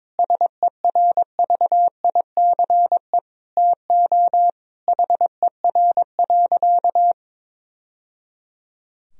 Morse code, heard as a keyed tone, about 22 wpm, 700 Hz. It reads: SERVICE TO HER.
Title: Sentences from Top 1000 Words - Review Part 4 - 22wpm